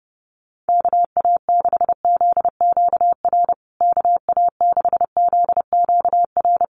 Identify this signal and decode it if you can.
Morse code, heard as a keyed tone, about 28 wpm, 700 Hz. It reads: KA6ZQR KA6ZQR
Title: USA Call Signs Repeated 3x 30wpm